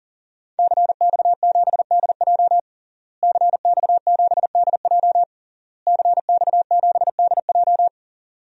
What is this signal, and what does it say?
Morse code, heard as a keyed tone, about 40 wpm, 700 Hz. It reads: CX7DJ CX7DJ CX7DJ
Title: DXCC-100 Call Signs - 5 Characters - Repeated 3x - 40wpm